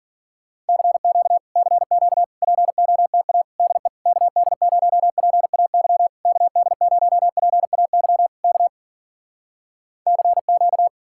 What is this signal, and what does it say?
Morse code, heard as a keyed tone, about 57 wpm, 700 Hz. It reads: CQ CQ POTA DE KD0PAY KD0PAY K CQ